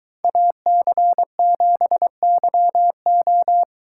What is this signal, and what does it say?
Morse code, heard as a keyed tone, about 22 wpm, 700 Hz. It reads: AC7YO